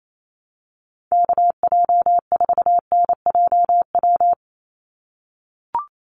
Morse code, heard as a keyed tone, about 28 wpm, 700 Hz. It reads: KJ4NJW
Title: USA Call Signs 28wpm